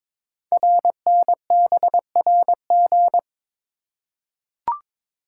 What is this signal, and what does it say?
Morse code, heard as a keyed tone, about 23 wpm, 700 Hz. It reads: RNBRG
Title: Code Groups - 5 Letters - 22wpm